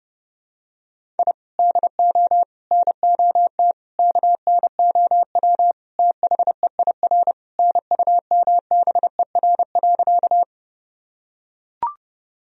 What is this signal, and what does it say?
Morse code, heard as a keyed tone, about 30 wpm, 700 Hz. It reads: I DO NOT KNOW THEIR NUMBER.
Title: Sentences from Top 1000 Words - Review Part 1 - 30wpm